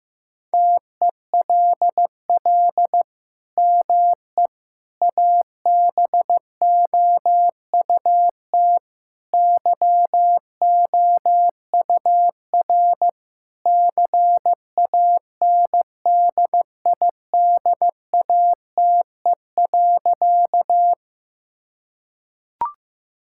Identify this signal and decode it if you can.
Morse code, heard as a keyed tone, about 15 wpm, 700 Hz. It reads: TELL ME ABOUT YOUR CANDIDATE.
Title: Sentences from Top 1600 Words 15wpm